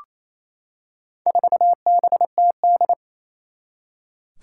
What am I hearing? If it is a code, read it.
Morse code, heard as a keyed tone, about 29 wpm, 700 Hz. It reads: 4BTD
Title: Four Challenging Letter-Number Combinations 28wpm